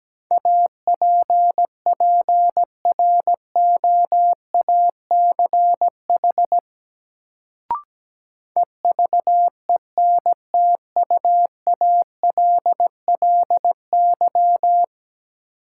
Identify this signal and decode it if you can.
Morse code, heard as a keyed tone, about 17 wpm, 700 Hz. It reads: APPROACH EVENTUALLY
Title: Sentences from Top 1000 Words - Review Part 6 - New Word Drill - 17wpm